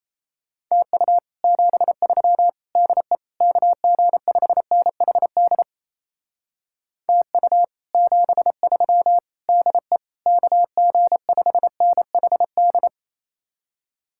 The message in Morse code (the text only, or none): TU 73 DE KG5NHD TU 73 DE KG5NHD